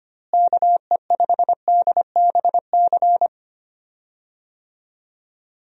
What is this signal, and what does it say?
Morse code, heard as a keyed tone, about 25 wpm, 700 Hz. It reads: KE5DBC
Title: USA Call Signs - 6 Characters - Repeated 2x - 25wpm